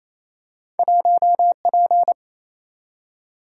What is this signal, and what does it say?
Morse code, heard as a keyed tone, about 28 wpm, 700 Hz. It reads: 1P
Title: Single-Letter Single-Number 28wpm